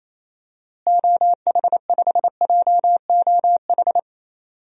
Morse code, heard as a keyed tone, about 27 wpm, 700 Hz. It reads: OH5JOH
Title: DXCC-100 Call Signs - 6 Characters - Repeated 2x - 28wpm